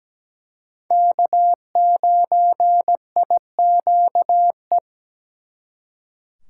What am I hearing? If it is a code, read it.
Morse code, heard as a keyed tone, about 20 wpm, 700 Hz. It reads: K9IQE
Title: Speed Racing - USA 1x3 Call Signs 17wpm